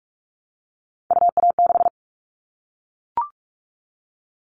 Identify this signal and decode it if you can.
Morse code, heard as a keyed tone, about 44 wpm, 700 Hz. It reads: UA6
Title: Callsign Prefixes for Top 25 DXCC - Repeated 3x - 45wpm